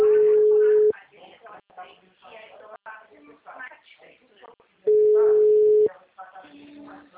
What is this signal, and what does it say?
Ringback tone (Europe, 425 Hz), with speech.